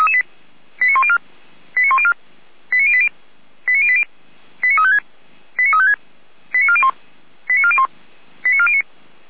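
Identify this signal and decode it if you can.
Five-tone selective call (ZVEI), heard as quick runs of five tones, tones of about 70 ms. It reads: 89104 89104 89080 89080 89356 89356 89401 89401 89409
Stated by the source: ZVEI five-tone call 89350 received by SDR.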